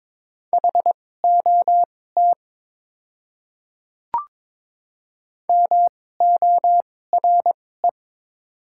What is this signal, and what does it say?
Morse code, heard as a keyed tone, about 22 wpm, spaced as for 16 wpm, 700 Hz. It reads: HOT MORE
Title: 2x Character Spacing - Top 100 Words - 22wpm